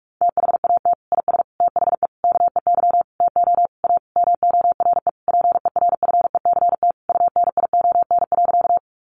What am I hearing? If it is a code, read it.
Morse code, heard as a keyed tone, about 44 wpm, 700 Hz. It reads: THAT IS THE KEY TO A MORE PERFECT UNION.
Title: Sentences from Top 1400 Words 45wpm